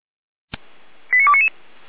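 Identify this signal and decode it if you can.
Five-tone selective call (ZVEI), heard as quick runs of five tones, tones of about 80 ms. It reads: 89299
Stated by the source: Received on an SDR; ZVEI five-tone code 89336.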